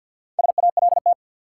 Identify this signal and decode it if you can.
Morse code, heard as a keyed tone, about 53 wpm, 700 Hz. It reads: SALT
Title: Sets of 5 Words - New Word Drill - 50wpm